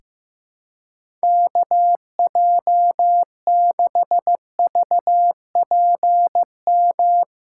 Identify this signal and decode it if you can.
Morse code, heard as a keyed tone, about 15 wpm, 700 Hz. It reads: KJ6VPM
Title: USA Call Signs - 6 Characters - 6x Character Spacing - 15wpm